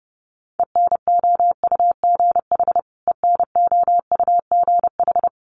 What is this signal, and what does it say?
Morse code, heard as a keyed tone, about 30 wpm, 700 Hz. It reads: ENOUGH ENOUGH
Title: Top 500 Words Repeated 30wpm